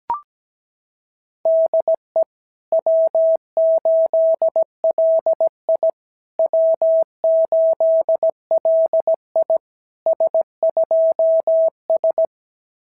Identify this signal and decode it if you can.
Morse code, heard as a keyed tone, about 16 wpm, 650 Hz. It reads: DE W8LI W8LI S2S
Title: SOTA QSO Practice 17wpm